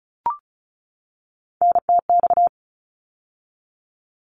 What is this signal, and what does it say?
Morse code, heard as a keyed tone, about 36 wpm, 700 Hz. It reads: NTX
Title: ARRL Sections 35wpm